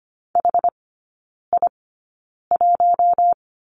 Morse code, heard as a keyed tone, about 25 wpm, spaced as for 9 wpm, 700 Hz. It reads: HI1
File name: Lesson 17e - Letter u - Three Characters 25wpm